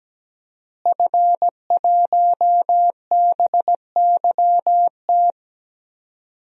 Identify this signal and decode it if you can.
Morse code, heard as a keyed tone, about 17 wpm, 700 Hz. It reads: F1BYT
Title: DXCC-100 Call Signs - 5 Characters - Repeated 3x - 17wpm